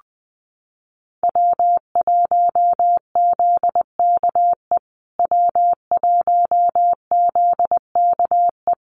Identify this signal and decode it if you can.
Morse code, heard as a keyed tone, about 20 wpm, 700 Hz. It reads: W1ZKE W1ZKE
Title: USA Call Signs Repeated 3x 20wpm